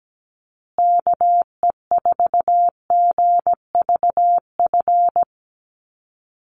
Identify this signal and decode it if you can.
Morse code, heard as a keyed tone, about 17 wpm, 700 Hz. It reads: KE4GVF
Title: USA Call Signs Repeated 3x 17wpm